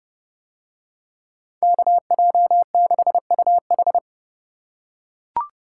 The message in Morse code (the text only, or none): KJ6UH